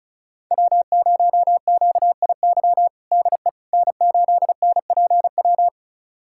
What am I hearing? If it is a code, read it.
Morse code, heard as a keyed tone, about 36 wpm, 700 Hz. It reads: W0QIY DE N8NPW
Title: USA Callsign DE Callsign - 5 Characters - 35wpm